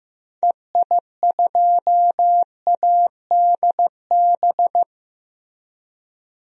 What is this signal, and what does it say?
Morse code, heard as a keyed tone, about 15 wpm, 700 Hz. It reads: EI2ADB
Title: DXCC-100 Call Signs - 6 Characters - Repeated 3x - 15wpm